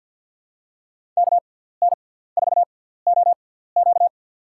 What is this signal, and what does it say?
Morse code, heard as a keyed tone, about 53 wpm, spaced as for 17 wpm, 700 Hz. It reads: KN4OQ
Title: USA Call Signs - 5 Characters - 6x Character Spacing - 50wpm